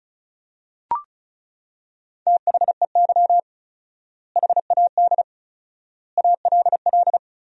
Morse code, heard as a keyed tone, about 35 wpm, 700 Hz. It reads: THEY HAD ALL
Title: 4x Word-Spacing - Sentences from Top 200 Words - 35wpm